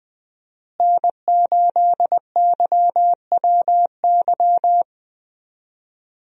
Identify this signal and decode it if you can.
Morse code, heard as a keyed tone, about 20 wpm, 700 Hz. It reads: N8YWY
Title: USA Call Signs - 5 Characters - Repeated 3x - 20wpm